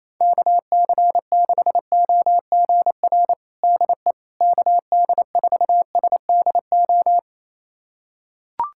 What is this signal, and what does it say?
Morse code, heard as a keyed tone, about 28 wpm, 700 Hz. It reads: KC6OGR DE KD4SDO
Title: USA Callsign DE Callsign 28wpm